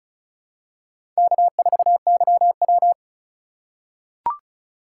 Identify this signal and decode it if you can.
Morse code, heard as a keyed tone, about 36 wpm, 700 Hz. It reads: K4YW
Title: USA Call Signs - 4 Characters - 3x Character Spacing - 35wpm